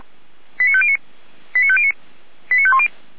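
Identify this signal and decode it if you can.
Five-tone selective call (ZVEI), heard as quick runs of five tones, tones of about 70 ms. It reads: 89509 89509 89510
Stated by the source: Five-tone ZVEI selective call received on SDR, from audio.f85360.feuerwehr-orlamuende.org (Feuerwehr Orlamünde fire brigade).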